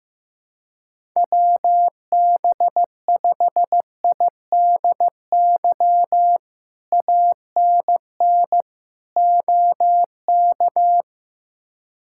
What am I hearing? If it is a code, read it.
Morse code, heard as a keyed tone, about 15 wpm, 700 Hz. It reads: WB5IDY ANN OK
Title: K1USN SST QSO Practice - 15wpm